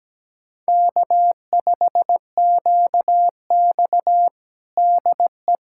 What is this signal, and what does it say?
Morse code, heard as a keyed tone, about 17 wpm, 700 Hz. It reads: K5QX DE
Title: USA Callsign DE Callsign - 4 Characters - 17wpm